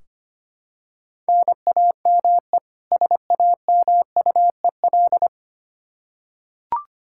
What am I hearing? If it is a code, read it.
Morse code, heard as a keyed tone, about 25 wpm, 700 Hz. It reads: NAME SAMUEL
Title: Practice QSO - Exchange 1 - Beginning - 25wpm